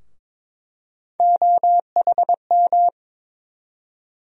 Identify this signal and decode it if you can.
Morse code, heard as a keyed tone, about 21 wpm, 700 Hz. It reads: OHM
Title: USA Call Sign Postfixes - 3 Letters 22wpm